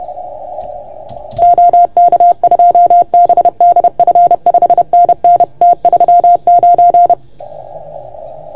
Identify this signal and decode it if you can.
Morse code, heard as a keyed tone, about 30 wpm, 690 Hz. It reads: OK2BDF 5NN T39
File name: Archiv / CUC381 / OK1FGD CUC 381